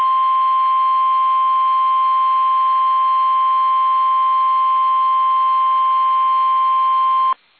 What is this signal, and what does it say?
Emergency Alert System (SAME) tones (the 1050 Hz attention tone). It is heard from 0.3 to 7.1 s.
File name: eas / KUSO / KUSO